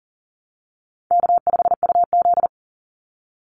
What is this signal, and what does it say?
Morse code, heard as a keyed tone, about 40 wpm, 700 Hz. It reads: K5UZ
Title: Easy Call Signs Repeated 3x - 40wpm